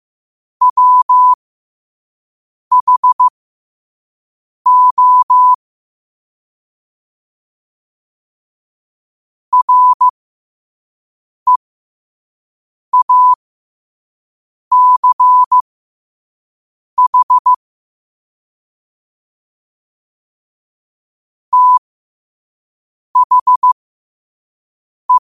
Morse code, heard as a keyed tone, about 15 wpm, spaced as for 5 wpm, 1000 Hz. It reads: WHO REACH THE
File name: qotdfiles / 5wpm